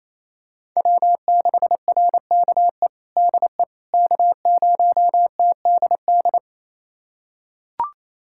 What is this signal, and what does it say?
Morse code, heard as a keyed tone, about 27 wpm, 700 Hz. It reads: W6RKE DE K0TDD
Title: USA Callsign DE Callsign - 5 Characters - 28wpm